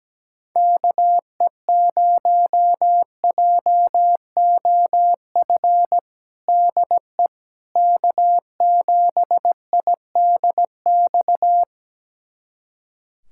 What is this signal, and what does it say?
Morse code, heard as a keyed tone, about 17 wpm, 700 Hz. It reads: KE0JOF DE K7IDX
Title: USA Callsign DE Callsign 17wpm